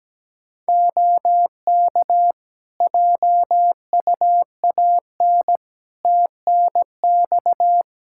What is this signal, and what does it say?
Morse code, heard as a keyed tone, about 17 wpm, 700 Hz. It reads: OK JUAN TNX
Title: K1USN SST QSO Practice 17wpm